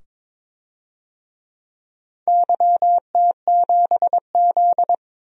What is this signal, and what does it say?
Morse code, heard as a keyed tone, about 22 wpm, 700 Hz. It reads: YT7Z
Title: DXCC-100 Call Signs - 4 Characters - 22wpm